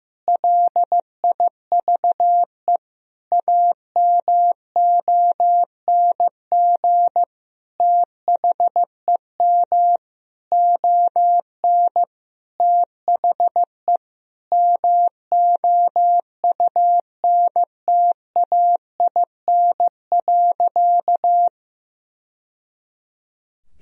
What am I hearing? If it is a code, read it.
Morse code, heard as a keyed tone, about 15 wpm, 700 Hz. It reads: LIVE AMONG THEM ON THE MOUNTAIN.
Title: Sentences from Top 1000 Words - Review Part 3 - 15wpm